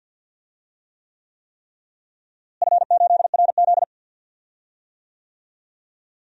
Morse code, heard as a keyed tone, about 52 wpm, 700 Hz. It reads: F8RZ
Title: DXCC-100 Call Signs - 4 Characters - Repeated 3x - 50wpm